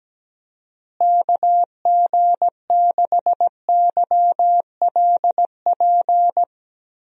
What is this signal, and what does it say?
Morse code, heard as a keyed tone, about 17 wpm, 700 Hz. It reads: KG6YLP
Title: USA Call Signs - 6 Characters - Repeated 3x - 17wpm